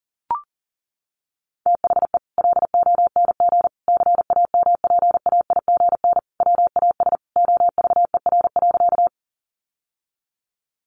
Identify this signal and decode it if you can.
Morse code, heard as a keyed tone, about 37 wpm, 700 Hz. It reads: THE LONG CAMPAIGN WAS OVER.
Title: Sentences from Top 1200 Words 40wpm